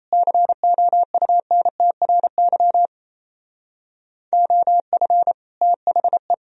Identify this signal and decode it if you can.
Morse code, heard as a keyed tone, about 36 wpm, 700 Hz. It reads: COUNTRY OF THE